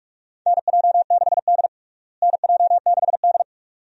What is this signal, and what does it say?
Morse code, heard as a keyed tone, about 47 wpm, 700 Hz. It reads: NJ6D NJ6D
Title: USA Call Signs - 4 Characters - Repeated 2x - 45wpm